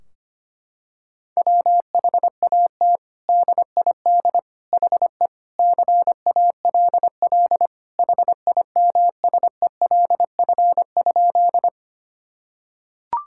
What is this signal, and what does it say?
Morse code, heard as a keyed tone, about 25 wpm, 700 Hz. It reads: WHAT DID HE CALL HIMSELF?
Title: Sentences from Top 1600 Words 25wpm